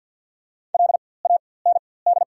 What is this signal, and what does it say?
Morse code, heard as a keyed tone, about 54 wpm, spaced as for 24 wpm, 700 Hz. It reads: LAND
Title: 4x Character Spacing - Top 300 Words - 50wpm